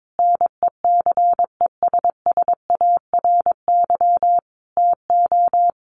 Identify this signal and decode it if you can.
Morse code, heard as a keyed tone, about 22 wpm, 700 Hz. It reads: NECESSARY TO
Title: Sentences from Top 800 Words 22wpm